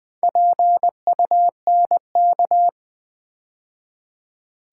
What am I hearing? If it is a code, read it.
Morse code, heard as a keyed tone, about 20 wpm, 700 Hz. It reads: PUNK
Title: Build A Word - 2 Letters At A Time - 4-Letter Words - 20wpm